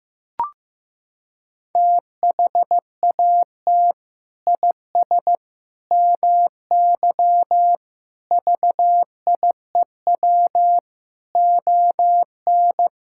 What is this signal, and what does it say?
Morse code, heard as a keyed tone, about 15 wpm, 700 Hz. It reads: THAT IS MY VIEW ON
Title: Sentences from Top 1000 Words - Review Part 6 - 15wpm